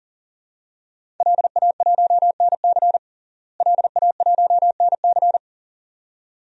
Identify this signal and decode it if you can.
Morse code, heard as a keyed tone, about 40 wpm, 700 Hz. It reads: LA1NC LA1NC
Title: DXCC-100 Call Signs - 5 Characters - Repeated 2x - 40wpm